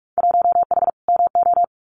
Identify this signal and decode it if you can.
Morse code, heard as a keyed tone, about 44 wpm, 700 Hz. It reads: 1H MO
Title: Winter Field Day Exchanges 45wpm